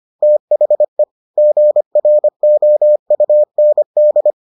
Morse code, heard as a keyed tone, about 25 wpm, 600 Hz. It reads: THE GROUND